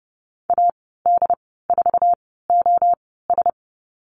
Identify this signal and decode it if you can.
Morse code, heard as a keyed tone, about 30 wpm, spaced as for 17 wpm, 700 Hz. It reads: AD4OS
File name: Lesson 14g - Number 2 - Call Signs 30wpm